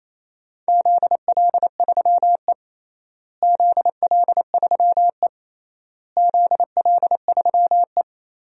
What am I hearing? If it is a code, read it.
Morse code, heard as a keyed tone, about 27 wpm, 700 Hz. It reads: ZL3E ZL3E ZL3E
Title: DXCC-100 Call Signs - 4 Characters - Repeated 3x - 28wpm